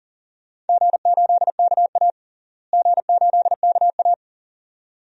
Morse code, heard as a keyed tone, about 40 wpm, 700 Hz. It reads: G8KA G8KA
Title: DXCC-100 Call Signs - 4 Characters - Repeated 2x - 40wpm